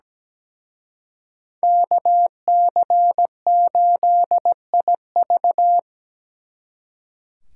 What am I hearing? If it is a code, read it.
Morse code, heard as a keyed tone, about 20 wpm, 700 Hz. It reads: KC8IV
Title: Speed Racing - USA 2x2 Call Signs 17wpm